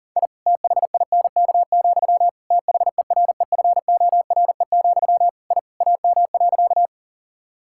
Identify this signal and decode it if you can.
Morse code, heard as a keyed tone, about 40 wpm, 700 Hz. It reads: I THINK, THEREFORE, I AM.
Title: Sentences from Top 1000 Words - Review Part 5 - 40wpm